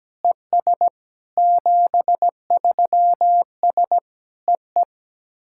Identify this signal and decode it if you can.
Morse code, heard as a keyed tone, about 17 wpm, 700 Hz. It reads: ES 73S EE
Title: SOTA QSO Practice 17wpm